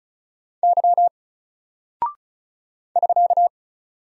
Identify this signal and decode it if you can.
Morse code, heard as a keyed tone, about 36 wpm, 700 Hz. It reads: Y <SK>
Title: Single Character (Letter:Number:Punctuation:Prosign) Alt 35wpm